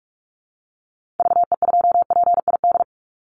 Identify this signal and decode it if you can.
Morse code, heard as a keyed tone, about 44 wpm, 700 Hz. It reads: VE2PID
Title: Call Signs Repeated 3x - 45wpm